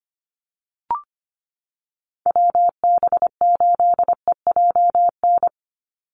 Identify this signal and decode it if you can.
Morse code, heard as a keyed tone, about 25 wpm, 700 Hz. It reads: WB8EJN
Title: Call Signs Repeated 3x - 25wpm